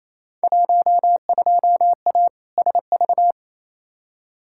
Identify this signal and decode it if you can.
Morse code, heard as a keyed tone, about 28 wpm, 700 Hz. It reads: 12A SV
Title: Field Day Exchanges Encore - 28wpm